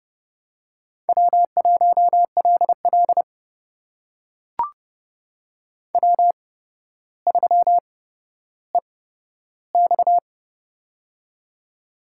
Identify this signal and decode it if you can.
Morse code, heard as a keyed tone, about 30 wpm, spaced as for 13 wpm, 700 Hz. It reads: W1LL W3EX
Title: USA Call Signs - 4 Characters - 8x Character Spacing - 30wpm